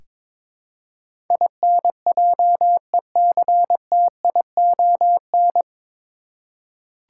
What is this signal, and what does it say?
Morse code, heard as a keyed tone, about 22 wpm, 700 Hz. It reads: INJECTION
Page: Morse Code Ninja practice 22wpm